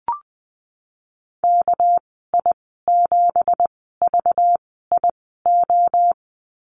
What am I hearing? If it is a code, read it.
Morse code, heard as a keyed tone, about 20 wpm, spaced as for 15 wpm, 700 Hz. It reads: KI7VIO
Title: 207 Lesson 33g - Letter j - Call Signs 20wpm